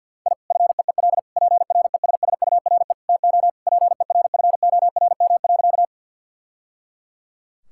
Morse code, heard as a keyed tone, about 50 wpm, 700 Hz. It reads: I FEEL PRESSURE TO PERFORM.
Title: Sentences from Top 1000 Words - Review Part 6 - 50wpm